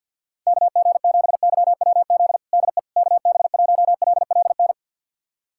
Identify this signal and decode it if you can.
Morse code, heard as a keyed tone, about 48 wpm, 700 Hz. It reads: KG7YWZ DE KB1LLN
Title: USA Callsign DE Callsign - 6 Characters - 50wpm